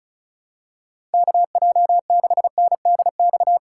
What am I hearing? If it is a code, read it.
Morse code, heard as a keyed tone, about 34 wpm, 700 Hz. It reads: KJ6NDX
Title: USA Call Signs - 6 Characters - 4x Character Spacing - 35wpm